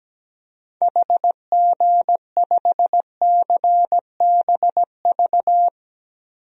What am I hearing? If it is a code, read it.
Morse code, heard as a keyed tone, about 17 wpm, 700 Hz. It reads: HG5CBV
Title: DXCC-100 Call Signs - 6 Characters - Repeated 2x - 17wpm